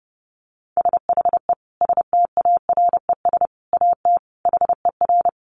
Morse code, heard as a keyed tone, about 30 wpm, 700 Hz. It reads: SHE STARES AT HER
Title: Easy Phrases 30wpm